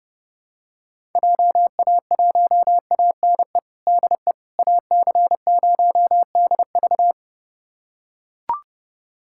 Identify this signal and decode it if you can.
Morse code, heard as a keyed tone, about 32 wpm, 690 Hz. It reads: JA1ANE DE AC0DV
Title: USA Callsign DE Callsign 30wpm